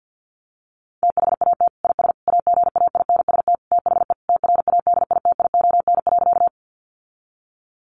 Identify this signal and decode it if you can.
Morse code, heard as a keyed tone, about 50 wpm, 700 Hz. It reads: THAT IS AGAINST THE TRADITION.
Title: Sentences from Top 1500 Words 50wpm